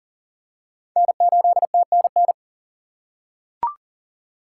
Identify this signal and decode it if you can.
Morse code, heard as a keyed tone, about 37 wpm, 700 Hz. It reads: N8TNN
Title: USA Call Signs - 6x Character Spacing - 40wpm